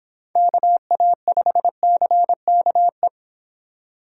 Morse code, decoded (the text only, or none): KA5CKE